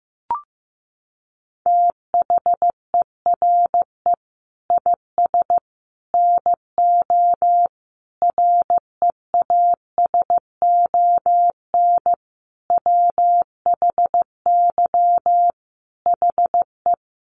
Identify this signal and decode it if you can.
Morse code, heard as a keyed tone, about 15 wpm, 700 Hz. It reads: THERE IS NO REASON WHY HE
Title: Sentences from Top 700 Words 15wpm